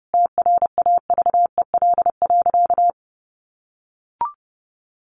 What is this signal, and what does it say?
Morse code, heard as a keyed tone, about 30 wpm, 700 Hz. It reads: TRAVEL.
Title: A Sound Of Thunder 30wpm